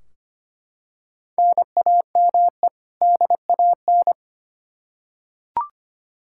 Morse code, heard as a keyed tone, about 25 wpm, 700 Hz. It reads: NAME DAN
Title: Practice QSO - Exchange 1 - Beginning - 25wpm